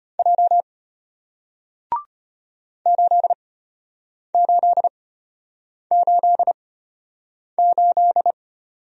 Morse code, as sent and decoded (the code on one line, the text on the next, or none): .--- / ---.. ---.. ---.. ---..
J 8888